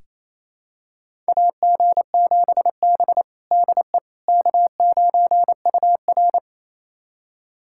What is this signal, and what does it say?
Morse code, heard as a keyed tone, about 28 wpm, 700 Hz. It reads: AG7B DE K9UR
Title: USA Callsign DE Callsign - 4 Characters - 28wpm